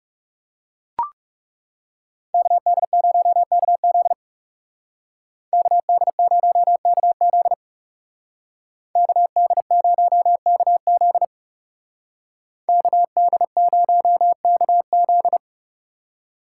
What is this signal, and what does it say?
Morse code, heard as a keyed tone, about 39 wpm, 700 Hz. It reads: KD0KZ KD0KZ KD0KZ KD0KZ